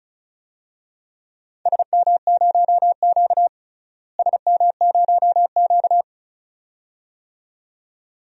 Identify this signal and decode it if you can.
Morse code, heard as a keyed tone, about 36 wpm, 700 Hz. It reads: SM0Q SM0Q
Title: DXCC-100 Call Signs - 4 Characters - Repeated 2x - 35wpm